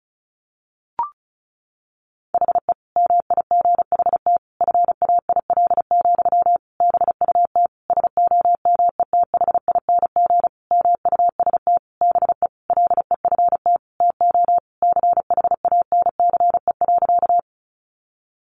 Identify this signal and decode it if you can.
Morse code, heard as a keyed tone, about 34 wpm, 700 Hz. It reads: HE MIGHT FAIL, BUT SOMETHING MUST BE LEFT TO CHANCE.
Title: Sentences from Top 1100 Words 35wpm